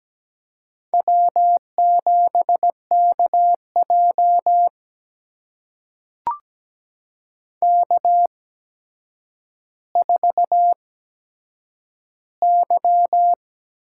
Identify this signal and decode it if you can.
Morse code, heard as a keyed tone, about 17 wpm, spaced as for 8 wpm, 700 Hz. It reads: W7KJ K4Y
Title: USA Call Signs - 4 Characters - 8x Character Spacing - 17wpm